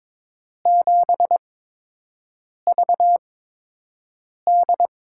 Morse code, heard as a keyed tone, about 22 wpm, spaced as for 6 wpm, 700 Hz. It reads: 7VD
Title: USA Call Signs - 8x Character Spacing - 22wpm